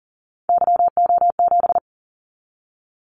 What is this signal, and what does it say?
Morse code, heard as a keyed tone, about 38 wpm, 700 Hz. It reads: YO7
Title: Callsign Prefixes for Top 25 DXCC - 40wpm